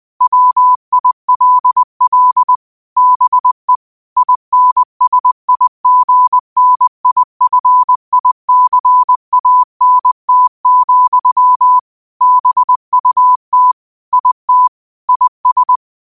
Morse code, heard as a keyed tone, about 20 wpm, 1000 Hz. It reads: WILL BE INSIGNIFICANT, BUT IT IS
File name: qotdfiles / 20wpm